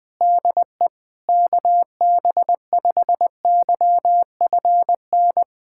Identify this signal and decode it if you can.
Morse code, heard as a keyed tone, about 20 wpm, 700 Hz. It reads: DE KB5YFN
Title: SOTA QSO Practice 20wpm